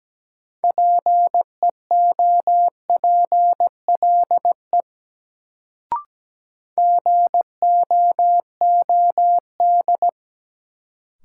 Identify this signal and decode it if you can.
Morse code, heard as a keyed tone, about 17 wpm, 700 Hz. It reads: PEOPLE GOOD
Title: Sets of 2 Words - New Word Drill - 17wpm